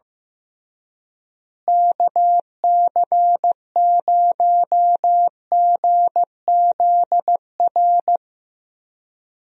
Morse code, heard as a keyed tone, about 15 wpm, 700 Hz. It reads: KC0GZR
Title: USA Call Signs - 6 Characters - Repeated 2x - 15wpm